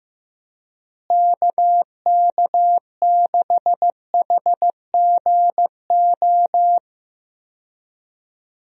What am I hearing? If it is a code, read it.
Morse code, heard as a keyed tone, about 15 wpm, 700 Hz. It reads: KK6HGO
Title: USA Call Signs - 6 Characters - Repeated 2x - 15wpm